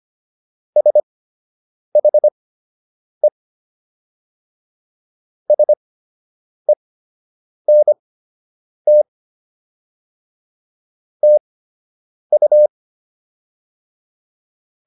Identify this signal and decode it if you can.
Morse code, heard as a keyed tone, about 25 wpm, spaced as for 8 wpm, 600 Hz. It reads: SHE SENT TU